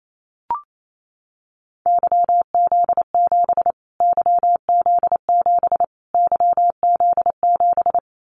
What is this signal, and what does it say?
Morse code, heard as a keyed tone, about 28 wpm, 700 Hz. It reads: YZ7 YZ7 YZ7
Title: Callsign Prefixes for Top 25 DXCC - Repeated 3x - 28wpm